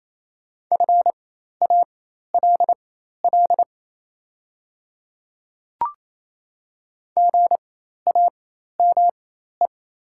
Morse code, heard as a keyed tone, about 28 wpm, spaced as for 13 wpm, 700 Hz. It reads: FALL GAME
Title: 4x Character Spacing - Top 500 Words - 28wpm